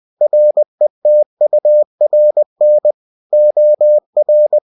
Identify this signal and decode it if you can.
Morse code, heard as a keyed tone, about 20 wpm, 600 Hz. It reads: RETURN OR
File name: cw-books / winnie-the-pooh / winnie-20wpm-page-0117